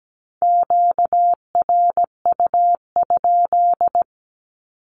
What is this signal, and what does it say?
Morse code, heard as a keyed tone, about 17 wpm, 700 Hz. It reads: QRU?
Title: Q-Codes with Spoken Meaning - 17wpm